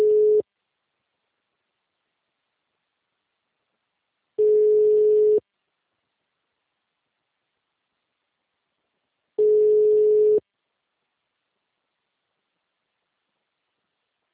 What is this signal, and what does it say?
Ringback tone (Europe, 425 Hz).
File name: out- -2017- - - .415